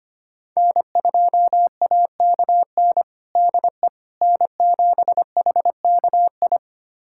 Morse code, heard as a keyed tone, about 25 wpm, 700 Hz. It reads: N2AKN DE N7HKI
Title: USA Callsign DE Callsign - 5 Characters - 25wpm